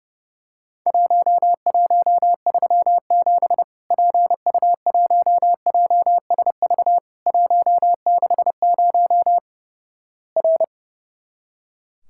Morse code, heard as a keyed tone, about 31 wpm, 700 Hz. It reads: 1137 PU1JSV 160 R
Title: Worked All Europe DX Contest - QTC Exchange - 30wpm